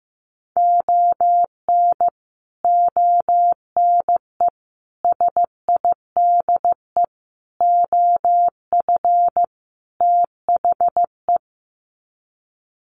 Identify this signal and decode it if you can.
Morse code, heard as a keyed tone, about 15 wpm, 700 Hz. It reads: ON ONE SIDE OF THE
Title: Sets of 5 Words 15wpm